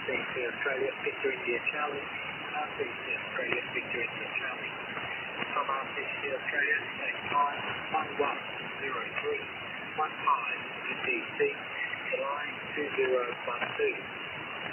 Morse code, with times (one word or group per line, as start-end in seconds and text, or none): none